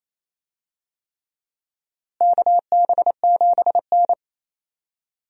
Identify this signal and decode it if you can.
Morse code, heard as a keyed tone, about 28 wpm, 700 Hz. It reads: KB7N
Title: USA Call Signs - 4 Characters - Repeated 3x - 28wpm